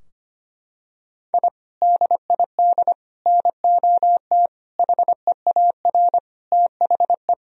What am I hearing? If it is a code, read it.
Morse code, heard as a keyed tone, about 25 wpm, 700 Hz. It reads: I DID NOT HEAR THE
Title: Sentences from Top 1000 Words - Review Part 2 - 25wpm